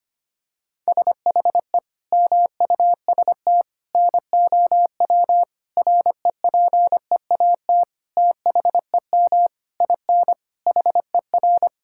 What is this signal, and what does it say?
Morse code, heard as a keyed tone, about 25 wpm, 700 Hz. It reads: SHE MUST NOW REPEAT THEM IN HER
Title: Sentences from Top 1600 Words 25wpm